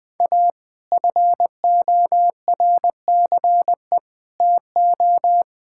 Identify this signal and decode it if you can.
Morse code, heard as a keyed tone, about 20 wpm, 700 Hz. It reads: A FORCE TO
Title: Sets of 8 Words - 20wpm